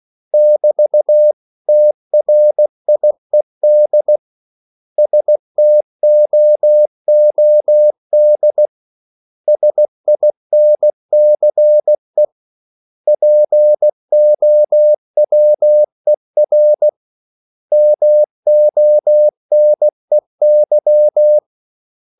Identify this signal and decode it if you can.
Morse code, heard as a keyed tone, about 16 wpm, 600 Hz. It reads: <BT> TRIED STOOD SINCE POWER MONEY
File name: CW / Practice / 500MostCommonEnglishWords16